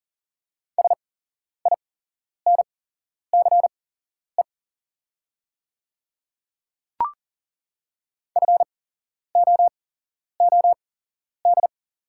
Morse code, heard as a keyed tone, about 39 wpm, spaced as for 11 wpm, 690 Hz. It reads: SINCE FOOD
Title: 8x Character Spacing - Top 500 Words - 40wpm